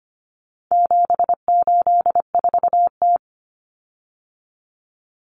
Morse code, heard as a keyed tone, about 25 wpm, 700 Hz. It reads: 784T
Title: Common Cut Numbers - 4-Digit 25wpm